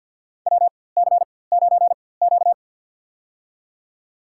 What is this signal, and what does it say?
Morse code, heard as a keyed tone, about 53 wpm, spaced as for 24 wpm, 700 Hz. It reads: WC9Q
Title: USA Call Signs - 4x Character Spacing - 50wpm